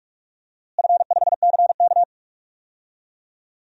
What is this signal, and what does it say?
Morse code, heard as a keyed tone, about 42 wpm, 700 Hz. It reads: F5CK
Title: DXCC-100 Call Signs - 4 Characters - Repeated 2x - 45wpm